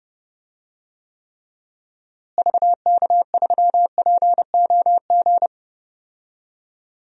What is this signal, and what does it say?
Morse code, heard as a keyed tone, about 31 wpm, 700 Hz. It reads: VK3POG
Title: DXCC-100 Call Signs - 6 Characters - Repeated 2x - 30wpm